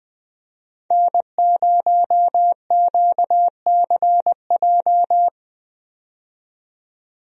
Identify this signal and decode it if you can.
Morse code, heard as a keyed tone, about 20 wpm, 700 Hz. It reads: N0QCJ
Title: USA Call Signs - 5 Characters - 20wpm